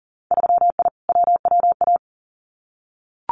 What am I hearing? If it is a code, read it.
Morse code, heard as a keyed tone, about 38 wpm, 700 Hz. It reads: 3I WWA
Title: Winter Field Day Exchanges Encore 40wpm